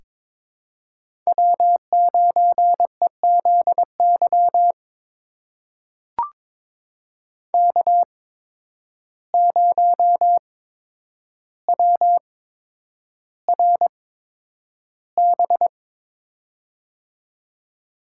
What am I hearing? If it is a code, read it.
Morse code, heard as a keyed tone, about 22 wpm, spaced as for 9 wpm, 700 Hz. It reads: W9EZY K0WRB
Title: USA Call Signs - 5 Characters - 8x Character Spacing - 22wpm